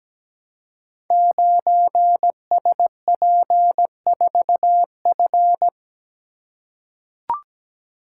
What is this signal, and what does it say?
Morse code, heard as a keyed tone, about 17 wpm, 700 Hz. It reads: 9SP4F
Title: Code Groups - 5 Letter:Numbers - 17wpm